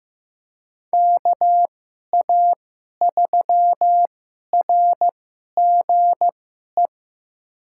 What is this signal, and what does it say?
Morse code, heard as a keyed tone, about 15 wpm, spaced as for 11 wpm, 700 Hz. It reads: KA3RGE
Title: USA Call Signs - 6 Characters - 2x Character Spacing - 15wpm